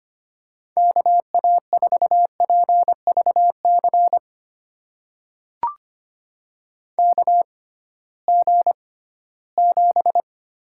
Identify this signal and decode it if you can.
Morse code, heard as a keyed tone, about 25 wpm, spaced as for 16 wpm, 700 Hz. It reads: KA4PVC KG7
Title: USA Call Signs - 6 Characters - 6x Character Spacing - 25wpm